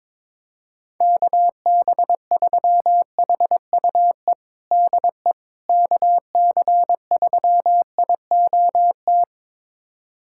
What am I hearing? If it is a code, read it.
Morse code, heard as a keyed tone, about 22 wpm, 700 Hz. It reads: KB3HUE DE KC3IOT
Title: USA Callsign DE Callsign - 6 Characters - 22wpm